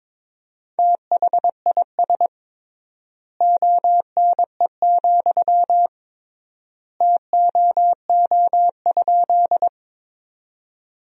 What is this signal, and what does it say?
Morse code, heard as a keyed tone, about 22 wpm, 700 Hz. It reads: THIS ONE, TOO?
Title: 3x Word-Spacing - Sentences from Top 200 Words - 22wpm